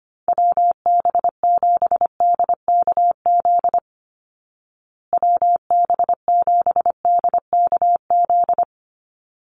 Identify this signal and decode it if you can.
Morse code, heard as a keyed tone, about 25 wpm, 700 Hz. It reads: WB7DKZ WB7DKZ
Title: Call Signs Repeated 3x - 25wpm